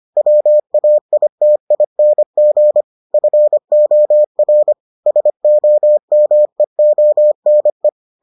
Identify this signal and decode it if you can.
Morse code, heard as a keyed tone, about 25 wpm, 600 Hz. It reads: WAITING FOR SOMEONE